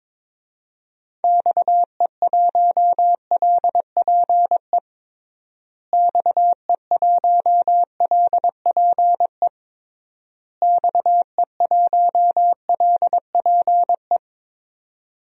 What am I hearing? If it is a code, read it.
Morse code, heard as a keyed tone, about 22 wpm, 700 Hz. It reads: XE1LPE XE1LPE XE1LPE
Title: DXCC-100 Call Signs - 6 Characters - Repeated 3x - 22wpm